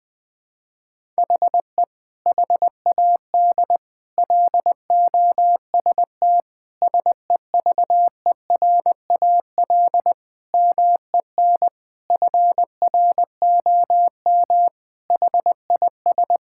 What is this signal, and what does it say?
Morse code, heard as a keyed tone, about 20 wpm, 700 Hz. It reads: HE HAD LOST SEVERAL MEN FROM HIS
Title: Sentences from Top 1600 Words 20wpm